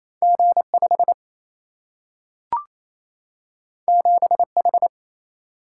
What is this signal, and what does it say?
Morse code, heard as a keyed tone, about 28 wpm, 700 Hz. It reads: G5 7H
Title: Single-Letter Single-Number 28wpm